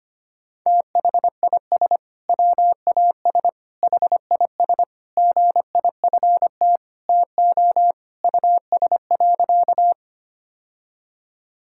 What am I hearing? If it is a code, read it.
Morse code, heard as a keyed tone, about 25 wpm, 700 Hz. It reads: THIS WAS HIS GIFT TO US.
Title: Sentences from Top 1000 Words - Review Part 6 - 25wpm